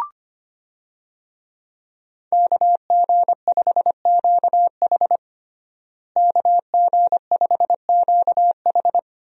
Morse code, heard as a keyed tone, about 25 wpm, 700 Hz. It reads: KG5QH KG5QH
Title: USA Call Signs - 5 Characters - Repeated 3x - 25wpm